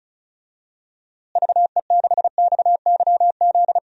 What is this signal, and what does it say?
Morse code, heard as a keyed tone, about 36 wpm, 700 Hz. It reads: VE6XYZ
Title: DXCC-100 Call Signs - 6 Characters - Repeated 3x - 35wpm